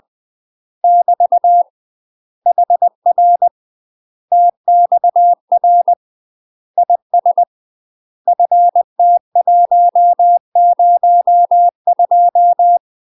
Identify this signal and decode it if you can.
Morse code, heard as a keyed tone, about 20 wpm, 700 Hz. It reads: <BT> HR TXR IS FT102
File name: UK MASTER QSOs TRAINING/20WPM